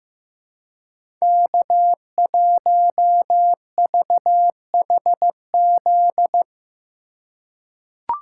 Morse code, heard as a keyed tone, about 15 wpm, 700 Hz. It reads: K1VHZ
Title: USA Call Signs - 4x Character Spacing - 15wpm